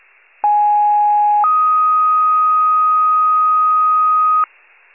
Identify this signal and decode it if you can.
Two-tone page, heard as a one-second tone, then a three-second tone, about 830 then 1250 Hz. It is heard from 0.4 to 4.4 s.
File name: pager / COUNTY FIRE / wcalert